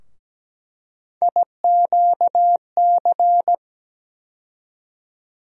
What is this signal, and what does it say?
Morse code, heard as a keyed tone, about 17 wpm, 700 Hz. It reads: IQC
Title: USA Call Sign Postfixes - 3 Letters 17wpm